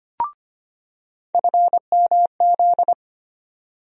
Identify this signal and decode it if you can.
Morse code, heard as a keyed tone, about 25 wpm, 700 Hz. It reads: FMZ
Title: USA Call Sign Postfixes - 3 Letters 25wpm